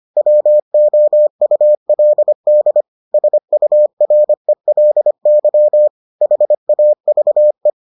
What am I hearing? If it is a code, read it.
Morse code, heard as a keyed tone, about 25 wpm, 600 Hz. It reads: WOULD SURELY HAVE